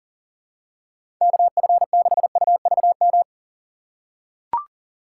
Morse code, heard as a keyed tone, about 40 wpm, 700 Hz. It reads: KF6UVM
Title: USA Call Signs - 2x Character Spacing - 40wpm